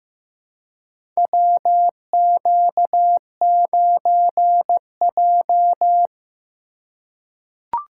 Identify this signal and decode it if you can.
Morse code, heard as a keyed tone, about 15 wpm, 700 Hz. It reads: WQ9J
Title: USA Call Signs - 4 Characters - 3x Character Spacing - 15wpm